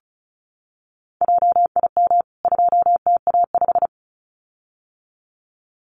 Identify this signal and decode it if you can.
Morse code, heard as a keyed tone, about 34 wpm, 700 Hz. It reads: JIM 2TA5
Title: CWT Exchange 35wpm